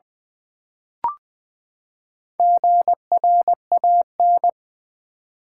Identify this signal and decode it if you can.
Morse code, heard as a keyed tone, about 20 wpm, 700 Hz. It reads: GRAN
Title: Build A Word - 4 Letters At A Time - 11-Letter Words - 20wpm